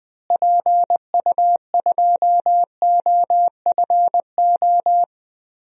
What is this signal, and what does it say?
Morse code, heard as a keyed tone, about 20 wpm, 700 Hz. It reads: PU2OFO
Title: DXCC-100 Call Signs - 6 Characters - Repeated 3x - 20wpm